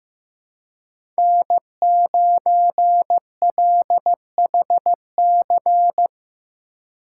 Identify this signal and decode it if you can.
Morse code, heard as a keyed tone, about 15 wpm, 700 Hz. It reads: N9LHC
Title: USA Call Signs - 4x Character Spacing - 15wpm